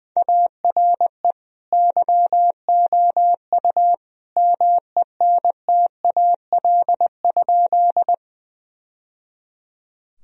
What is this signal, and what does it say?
Morse code, heard as a keyed tone, about 21 wpm, 700 Hz. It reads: ARE YOU MENTAL?
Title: Sentences from Top 1000 Words - Review Part 4 - 20wpm